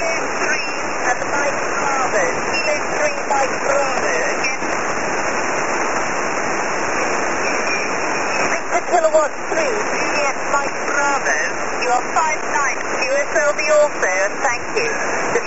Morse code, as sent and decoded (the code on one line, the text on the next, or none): none